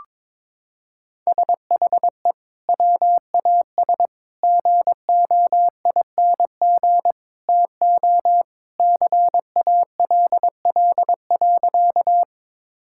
Morse code, heard as a keyed tone, about 22 wpm, 700 Hz. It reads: SHE WAS GOING TO CALL.
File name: practice / 22wpm / Sentences with 3-Word Phrases - 22wpm